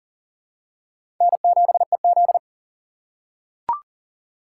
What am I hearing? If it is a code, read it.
Morse code, heard as a keyed tone, about 40 wpm, 700 Hz. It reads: N7EZ
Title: USA Call Signs - 4 Characters - 2x Character Spacing - 40wpm